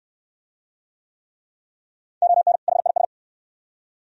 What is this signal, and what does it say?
Morse code, heard as a keyed tone, about 62 wpm, 700 Hz. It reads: ON HIS